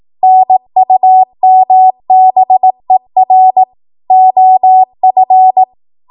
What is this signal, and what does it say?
Morse code, heard as a keyed tone, about 18 wpm, 750 Hz. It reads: NUMBER OF
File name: Morse / 18WPM